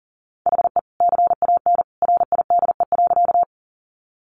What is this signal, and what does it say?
Morse code, heard as a keyed tone, about 40 wpm, 700 Hz. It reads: HE CAN RIDE.
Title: Sentences from Top 1200 Words 40wpm